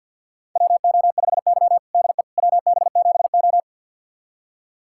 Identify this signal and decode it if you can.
Morse code, heard as a keyed tone, about 48 wpm, 700 Hz. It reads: WO5Y DE WB7O
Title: USA Callsign DE Callsign - 4 Characters - 50wpm